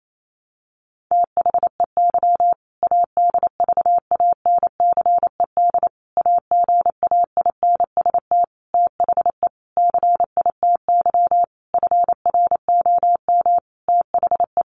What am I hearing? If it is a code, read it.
Morse code, heard as a keyed tone, about 28 wpm, 700 Hz. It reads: THEY ADVANCED AGAINST THE CITY FROM THE
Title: Sentences from Top 1200 Words 28wpm